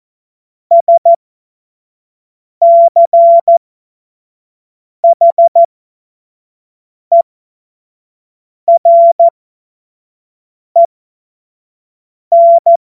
Morse code, heard as a keyed tone, about 14 wpm, spaced as for 5 wpm, 680 Hz. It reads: SCHEREN
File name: Morsekurs / Witze / Witz 18. April 2023 25 F70